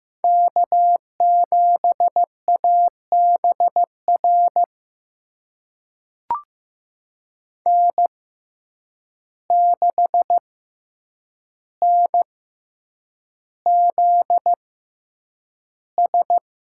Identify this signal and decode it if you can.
Morse code, heard as a keyed tone, about 15 wpm, spaced as for 8 wpm, 700 Hz. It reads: K7ABR N6NZS
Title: USA Call Signs - 5 Characters - 6x Character Spacing - 15wpm